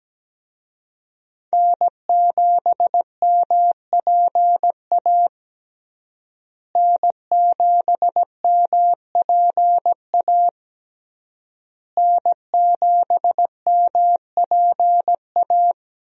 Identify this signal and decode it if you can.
Morse code, heard as a keyed tone, about 17 wpm, 700 Hz. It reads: N7MPA N7MPA N7MPA
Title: USA Call Signs - 5 Characters - Repeated 3x - 17wpm